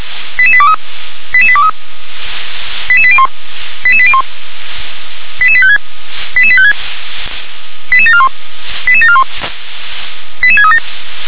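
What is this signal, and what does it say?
Five-tone selective call (ZVEI), heard as quick runs of five tones, tones of about 70 ms. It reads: 88823 88823 88801 88801 88856 88856 88741 88741 88637
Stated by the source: Five-tone ZVEI selective call received on an SDR.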